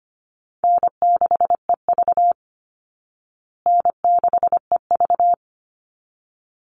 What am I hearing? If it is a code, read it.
Morse code, heard as a keyed tone, about 25 wpm, 700 Hz. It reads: N6EV N6EV
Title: Call Signs Repeated 3x - 25wpm